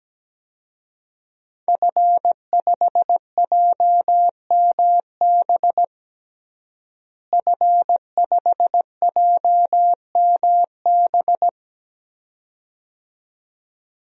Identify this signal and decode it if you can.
Morse code, heard as a keyed tone, about 17 wpm, 700 Hz. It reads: F5JMB F5JMB
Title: DXCC-100 Call Signs - 5 Characters - Repeated 2x - 17wpm